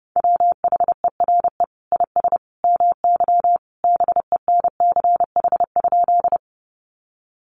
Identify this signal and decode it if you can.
Morse code, heard as a keyed tone, about 28 wpm, 700 Hz. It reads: WHERE IS MY BENCH?
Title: Sentences from Top 1400 Words 30wpm